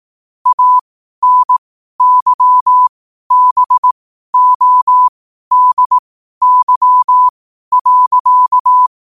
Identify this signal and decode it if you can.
Morse code, heard as a keyed tone, about 18 wpm, spaced as for 12 wpm, 1000 Hz. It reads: ANYBODY.
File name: qotdfiles / 12wpm